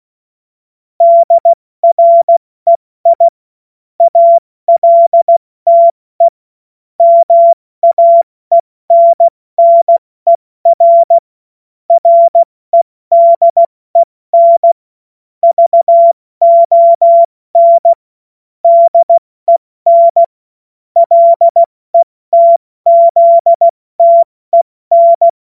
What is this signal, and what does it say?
Morse code, heard as a keyed tone, about 16 wpm, 680 Hz. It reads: DREI ALTE MAENNER REDEN VON DEN LETZTEN
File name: Morsekurs / Witze / Witz 04. Januar 2022, 70 F80